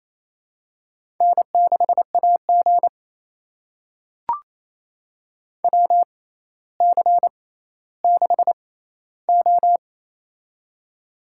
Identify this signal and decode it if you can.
Morse code, heard as a keyed tone, about 28 wpm, spaced as for 14 wpm, 700 Hz. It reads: N6AG WC6O
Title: USA Call Signs - 4 Characters - 6x Character Spacing - 28wpm